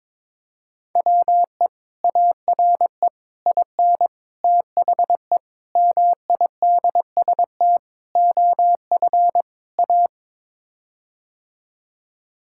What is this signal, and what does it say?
Morse code, heard as a keyed tone, about 22 wpm, 700 Hz. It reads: WE ARE IN THE MIDST OF A
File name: practice / 22wpm / Sets of 7 Words - 22wpm